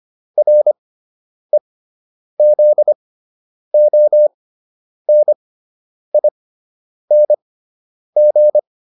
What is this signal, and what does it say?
Morse code, heard as a keyed tone, about 25 wpm, spaced as for 9 wpm, 600 Hz. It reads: REZONING